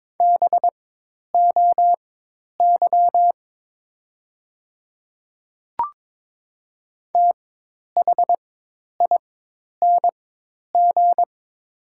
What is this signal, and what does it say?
Morse code, heard as a keyed tone, about 22 wpm, spaced as for 10 wpm, 700 Hz. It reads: BOY THING
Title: 4x Character Spacing - Top 200 Words - 22wpm